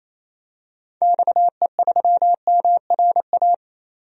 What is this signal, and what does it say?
Morse code, heard as a keyed tone, about 27 wpm, 700 Hz. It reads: XE3MRA
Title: DXCC-100 Call Signs - 6 Characters - 28wpm